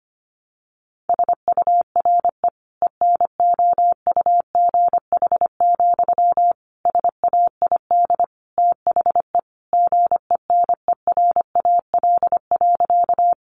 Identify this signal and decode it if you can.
Morse code, heard as a keyed tone, about 25 wpm, 700 Hz. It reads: SURE ENOUGH, SAID THE GENERAL.
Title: Sentences from Top 600 Words 25wpm